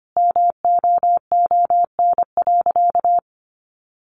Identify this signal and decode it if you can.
Morse code, heard as a keyed tone, about 25 wpm, 700 Hz. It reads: MOON.
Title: A Sound Of Thunder 25wpm